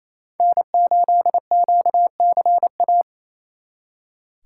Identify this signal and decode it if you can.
Morse code, heard as a keyed tone, about 35 wpm, 700 Hz. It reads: N8QCA